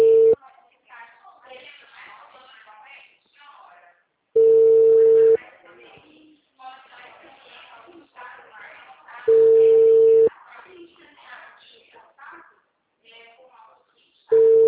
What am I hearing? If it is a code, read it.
Ringback tone (China, 450 Hz), with speech.